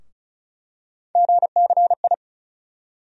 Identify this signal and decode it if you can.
Morse code, heard as a keyed tone, about 34 wpm, 700 Hz. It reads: GCI
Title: USA Call Sign Postfixes - 3 Letters 35wpm